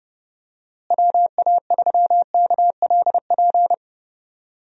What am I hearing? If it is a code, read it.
Morse code, heard as a keyed tone, about 32 wpm, 700 Hz. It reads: WA3KLP
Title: USA Call Signs - 6 Characters - Repeated 2x - 30wpm